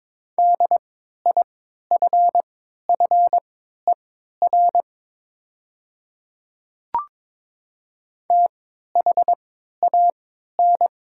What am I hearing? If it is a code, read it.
Morse code, heard as a keyed tone, about 22 wpm, spaced as for 12 wpm, 700 Hz. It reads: DIFFER THAN
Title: 3x Character Spacing - Top 200 Words - 22wpm